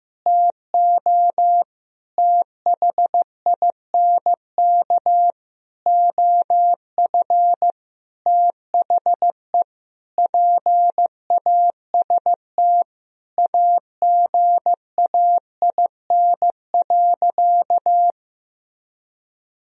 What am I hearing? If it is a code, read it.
Morse code, heard as a keyed tone, about 16 wpm, 700 Hz. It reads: TO THINK OF THE PAST AGAIN.
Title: Sentences from Top 1000 Words - Review Part 3 - 15wpm